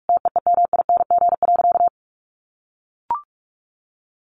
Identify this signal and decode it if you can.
Morse code, heard as a keyed tone, about 44 wpm, 700 Hz. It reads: TEEMING.
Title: A Sound Of Thunder 45wpm